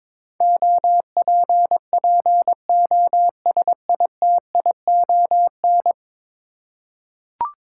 Morse code, heard as a keyed tone, about 22 wpm, 700 Hz. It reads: OPPOSITION
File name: practice / 22wpm / 10 Letter Words - First 3 Letters Given - 22wpm